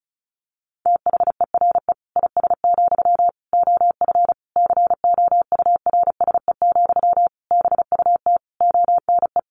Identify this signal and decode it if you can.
Morse code, heard as a keyed tone, about 34 wpm, 700 Hz. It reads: THERE IS, OF COURSE, BUT ONE
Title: Sentences from Top 1200 Words 35wpm